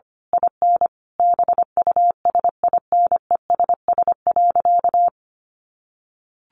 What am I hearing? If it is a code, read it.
Morse code, heard as a keyed tone, about 25 wpm, 700 Hz. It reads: IN BUSINESS.
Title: Sentences from Top 1300 Words 25wpm